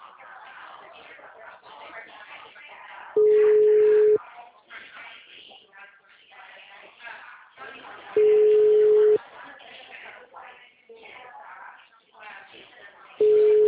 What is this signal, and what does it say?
Ringback tone (Europe, 425 Hz), with speech.